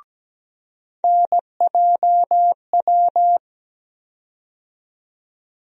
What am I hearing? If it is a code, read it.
Morse code, heard as a keyed tone, about 17 wpm, 700 Hz. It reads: NJW
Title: USA Call Sign Postfixes - 3 Letters 17wpm